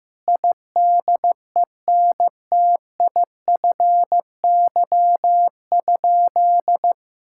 Morse code, heard as a keyed tone, about 15 wpm, 700 Hz. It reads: IDENTIFY?
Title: Sentences from Top 1000 Words - Review Part 6 - 15wpm